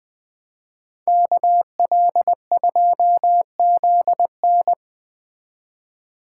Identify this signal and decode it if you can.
Morse code, heard as a keyed tone, about 20 wpm, 700 Hz. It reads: KL2ZN
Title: Speed Racing - USA 2x2 Call Signs 17wpm